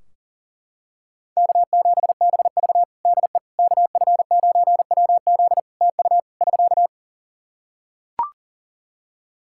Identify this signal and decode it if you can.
Morse code, heard as a keyed tone, about 37 wpm, 700 Hz. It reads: K7BV DE KF9WZ TU <SK>
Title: K1USN SST QSO Practice 40wpm